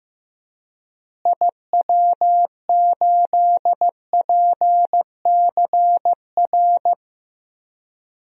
Morse code, heard as a keyed tone, about 15 wpm, 700 Hz. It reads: IW8PCR
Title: DXCC-100 Call Signs - 6 Characters - Repeated 3x - 15wpm